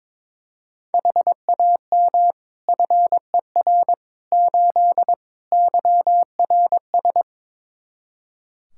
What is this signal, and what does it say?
Morse code, heard as a keyed tone, about 21 wpm, 700 Hz. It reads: HAM FER 8 YRS
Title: Practice QSO - Exchange 3 - More Info - 22wpm